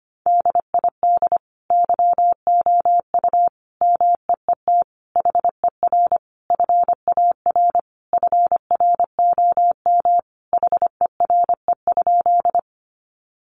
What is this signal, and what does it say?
Morse code, heard as a keyed tone, about 25 wpm, 690 Hz. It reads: DID YOU MEET HER FAR FROM HERE?
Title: Sentences from Top 1100 Words 25wpm